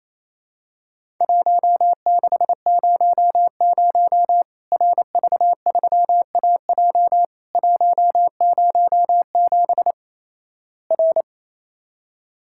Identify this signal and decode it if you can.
Morse code, heard as a keyed tone, about 28 wpm, 700 Hz. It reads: 1600 RV3AJ 107 R
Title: Worked All Europe DX Contest - QTC Exchange - 28wpm